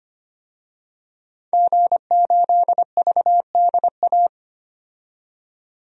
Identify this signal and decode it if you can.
Morse code, heard as a keyed tone, about 25 wpm, 700 Hz. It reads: G8VDA
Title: DXCC-100 Call Signs - 5 Characters - 25wpm